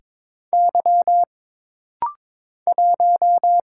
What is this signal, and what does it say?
Morse code, heard as a keyed tone, about 22 wpm, 700 Hz. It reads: Y 1
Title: Single Letter-Number Alt - Rapid-Fire 22wpm